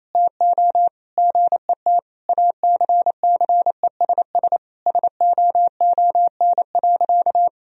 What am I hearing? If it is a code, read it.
Morse code, heard as a keyed tone, about 27 wpm, 700 Hz. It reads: TO GET ACCESS SOON.
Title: Sentences from Top 1000 Words - Review Part 4 - 28wpm